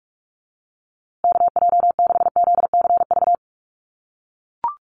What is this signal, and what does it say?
Morse code, heard as a keyed tone, about 44 wpm, 700 Hz. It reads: KJ6ZCV
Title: USA Call Signs Repeated 3x 45wpm